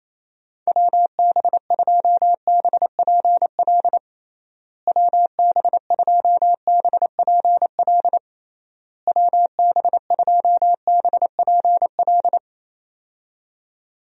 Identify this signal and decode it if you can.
Morse code, heard as a keyed tone, about 28 wpm, 700 Hz. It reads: WB2BPL WB2BPL WB2BPL
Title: USA Call Signs - 6 Characters - Repeated 3x - 28wpm